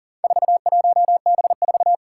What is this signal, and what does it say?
Morse code, heard as a keyed tone, about 40 wpm, 700 Hz. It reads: 41B4
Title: Four Challenging Letter-Number Combinations 40wpm